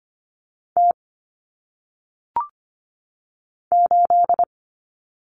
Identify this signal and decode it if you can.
Morse code, heard as a keyed tone, about 25 wpm, 700 Hz. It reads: T 8
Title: Common Cut Numbers - 1-Digit 25wpm